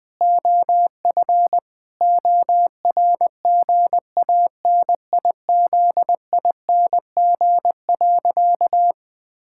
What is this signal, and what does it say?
Morse code, heard as a keyed tone, about 21 wpm, 700 Hz. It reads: OF ORGANIZING.
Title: Sentences with 7-Word Phrases - Spoken Beginning - 20wpm